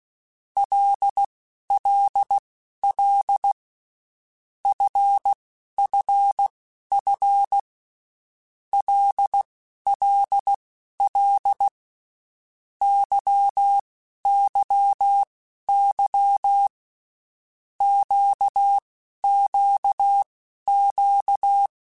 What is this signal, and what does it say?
Morse code, heard as a keyed tone, about 16 wpm, spaced as for 11 wpm, 800 Hz. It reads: LLL FFF LLL YYY QQQ
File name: Groups@10 / Groups@10 mp3s / LFCYQ@10